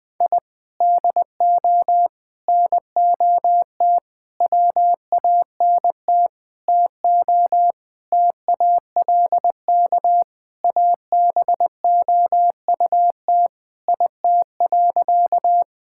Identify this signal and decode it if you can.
Morse code, heard as a keyed tone, about 20 wpm, 700 Hz. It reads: I DO NOT WANT TO TALK ABOUT IT.
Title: Sentences with 5-Word Phrases - Spoken Beginning - 20wpm